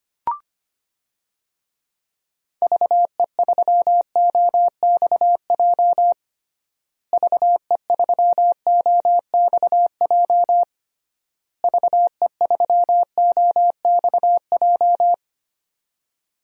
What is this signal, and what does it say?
Morse code, heard as a keyed tone, about 25 wpm, 700 Hz. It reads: VE3OXJ VE3OXJ VE3OXJ
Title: DXCC-100 Call Signs - 6 Characters - Repeated 3x - 25wpm